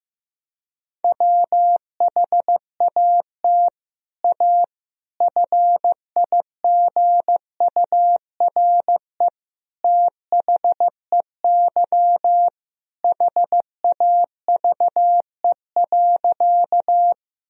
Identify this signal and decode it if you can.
Morse code, heard as a keyed tone, about 15 wpm, 700 Hz. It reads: WHAT A FIGURE THEY HAVE.
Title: Sentences from Top 1000 Words - Review Part 2 - 15wpm